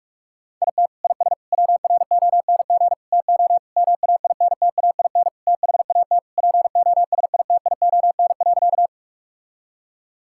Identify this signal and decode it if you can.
Morse code, heard as a keyed tone, about 46 wpm, 700 Hz. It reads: IT IS WRONG TO MAINTAIN THAT POSITION.
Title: Sentences from Top 1000 Words - Review Part 5 - 45wpm